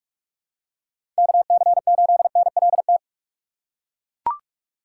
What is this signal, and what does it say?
Morse code, heard as a keyed tone, about 42 wpm, 700 Hz. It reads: KC8NLT
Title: USA Call Signs - 6 Characters - 4x Character Spacing - 45wpm